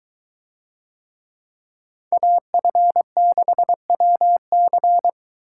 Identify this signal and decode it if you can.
Morse code, heard as a keyed tone, about 23 wpm, 700 Hz. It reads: AF6WC